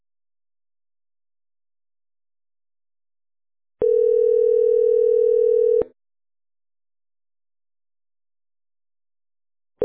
Ringback tone (North America), 440 + 480 Hz.